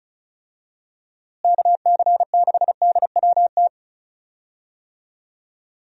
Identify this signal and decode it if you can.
Morse code, heard as a keyed tone, about 34 wpm, 700 Hz. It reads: KC6DWT
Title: USA Call Signs - 6 Characters - Repeated 2x - 35wpm